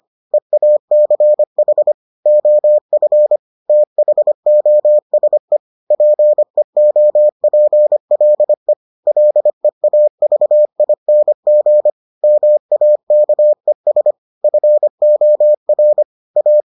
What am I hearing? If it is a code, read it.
Morse code, heard as a keyed tone, about 25 wpm, 600 Hz. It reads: EACH OF THOSE PEOPLE LEAVING MAKES FOR A